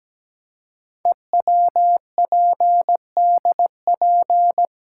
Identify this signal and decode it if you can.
Morse code, heard as a keyed tone, about 17 wpm, 700 Hz. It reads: EWPDP
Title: Code Groups - 5 Letters - 17wpm